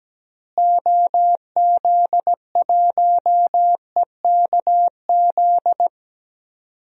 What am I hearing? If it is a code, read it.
Morse code, heard as a keyed tone, about 17 wpm, 700 Hz. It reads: OZ1EKZ
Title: DXCC-100 Call Signs - 6 Characters - Repeated 3x - 17wpm